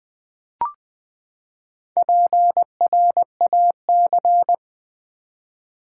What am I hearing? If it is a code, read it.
Morse code, heard as a keyed tone, about 20 wpm, 700 Hz. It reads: PRAC
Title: Build A Word - 4 Letters At A Time - 12-Letter Words - 20wpm